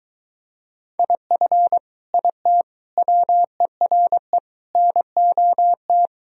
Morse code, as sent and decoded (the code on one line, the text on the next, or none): .. ..-. / .. - / .-- . .-. / -. --- -
IF IT WERE NOT